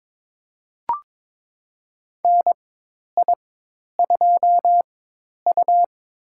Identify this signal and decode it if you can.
Morse code, heard as a keyed tone, about 22 wpm, spaced as for 10 wpm, 700 Hz. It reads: NI2U
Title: USA Call Signs - 4 Characters - 4x Character Spacing - 22wpm